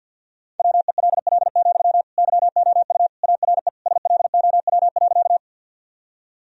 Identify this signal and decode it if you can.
Morse code, heard as a keyed tone, about 47 wpm, 700 Hz. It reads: WELL, YOU ARE SLOW.
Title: Sentences from Top 1000 Words - Review Part 2 - 50wpm